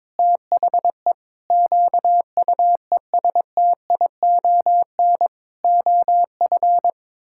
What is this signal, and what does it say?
Morse code, heard as a keyed tone, about 22 wpm, 700 Hz. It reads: THE QUESTION OF
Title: Sentences from Top 1600 Words 22wpm